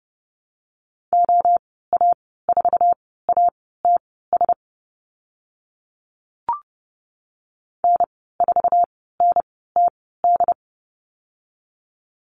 Morse code, heard as a keyed tone, about 30 wpm, spaced as for 17 wpm, 700 Hz. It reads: OA4ATS N4NTD
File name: course / 30:10wpm / 088 Lesson 15g - Number 5 - Call Signs 30wpm